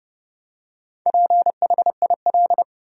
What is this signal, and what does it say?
Morse code, heard as a keyed tone, about 30 wpm, 700 Hz. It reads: PHIL
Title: USA Common Names - Max 4 Letters - 30wpm